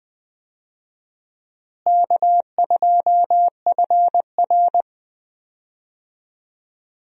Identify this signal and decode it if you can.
Morse code, heard as a keyed tone, about 20 wpm, 700 Hz. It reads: K2FR
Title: USA Call Signs - 4 Characters - Repeated 2x - 20wpm